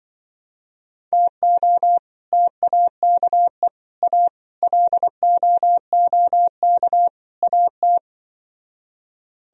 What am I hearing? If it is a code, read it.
Morse code, heard as a keyed tone, about 24 wpm, 700 Hz. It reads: TO TAKE A LOOK AT
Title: Speed-Racing - Sets of 5 Words 20wpm